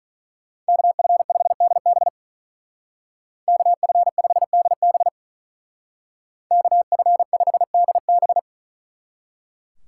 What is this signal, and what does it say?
Morse code, heard as a keyed tone, about 46 wpm, 700 Hz. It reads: KF5DB KF5DB KF5DB